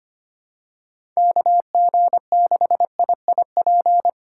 Morse code, heard as a keyed tone, about 25 wpm, 700 Hz. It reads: KG6IIP
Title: K1USN SST QSO Practice - 25wpm